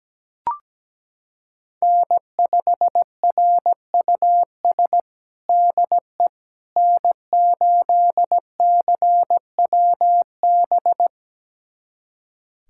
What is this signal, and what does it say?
Morse code, heard as a keyed tone, about 17 wpm, 700 Hz. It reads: N5RUS DE N8CWB
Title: USA Callsign DE Callsign - 5 Characters - 17wpm